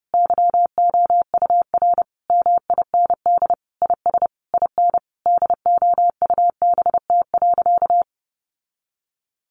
Morse code, heard as a keyed tone, about 30 wpm, 700 Hz. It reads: YOUR MIND IS IN DOUBT.
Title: Sentences from Top 1500 Words 30wpm